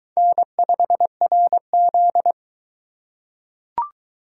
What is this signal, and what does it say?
Morse code, heard as a keyed tone, about 22 wpm, 700 Hz. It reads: N5RZ